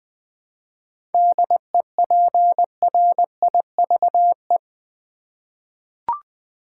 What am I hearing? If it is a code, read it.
Morse code, heard as a keyed tone, about 20 wpm, 700 Hz. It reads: DEPRIVE
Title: 7 Letter Words - First 3 Letters Given - 20wpm